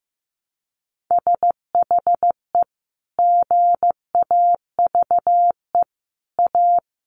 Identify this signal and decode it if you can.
Morse code, heard as a keyed tone, about 15 wpm, 700 Hz. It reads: SHE GAVE A
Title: Sentences from Top 1400 Words 15wpm